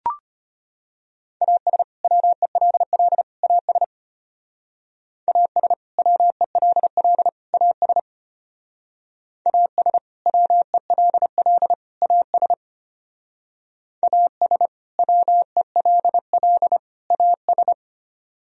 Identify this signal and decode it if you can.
Morse code, heard as a keyed tone, about 31 wpm, 700 Hz. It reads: AS WELL AS AS WELL AS AS WELL AS AS WELL AS
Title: Speed-Racing - Sets of 3 Words 25wpm